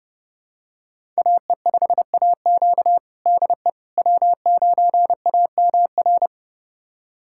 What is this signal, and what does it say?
Morse code, heard as a keyed tone, about 32 wpm, 700 Hz. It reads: AE5AQ DE W9AMR
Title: USA Callsign DE Callsign - 5 Characters - 30wpm